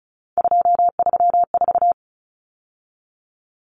Morse code, heard as a keyed tone, about 34 wpm, 700 Hz. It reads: 234
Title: Numbers - 3-Digit 35wpm